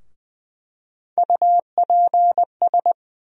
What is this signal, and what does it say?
Morse code, heard as a keyed tone, about 20 wpm, 700 Hz. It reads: UPS
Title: USA Call Sign Postfixes - 3 Letters 20wpm